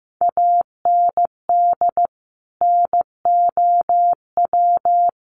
Morse code, heard as a keyed tone, about 15 wpm, 700 Hz. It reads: AND NOW
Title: Sentences from Top 1100 Words 15wpm